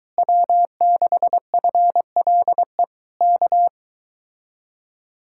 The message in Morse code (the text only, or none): W6FLE K